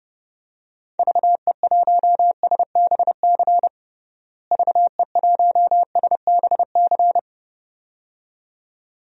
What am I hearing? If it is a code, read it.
Morse code, heard as a keyed tone, about 32 wpm, 700 Hz. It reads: VE1SBC VE1SBC
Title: DXCC-100 Call Signs - 6 Characters - Repeated 2x - 30wpm